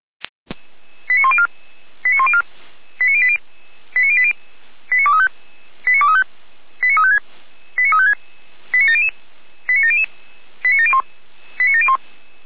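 Five-tone selective call (ZVEI), heard as quick runs of five tones, tones of about 70 ms. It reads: 89104 89104 89080 89080 89235 89235 89356 89356 89700 89701 89701
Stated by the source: Five-tone ZVEI selective call received on SDR.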